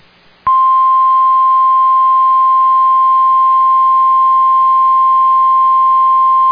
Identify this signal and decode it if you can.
Emergency Alert System (SAME) tones (the 1050 Hz attention tone). It is heard from 1.0 to 5.9 s.